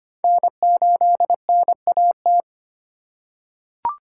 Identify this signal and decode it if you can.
Morse code, heard as a keyed tone, about 25 wpm, 700 Hz. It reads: N8NAT
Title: USA Call Signs - 4x Character Spacing - 25wpm